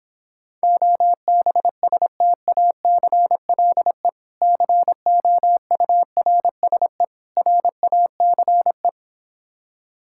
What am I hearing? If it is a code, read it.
Morse code, heard as a keyed tone, about 26 wpm, 700 Hz. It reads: OBSTACLE COURSE RACE